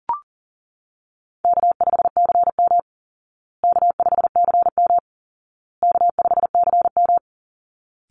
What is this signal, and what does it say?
Morse code, heard as a keyed tone, about 40 wpm, 700 Hz. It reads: K5CM K5CM K5CM
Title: Easy Call Signs Repeated 3x - 40wpm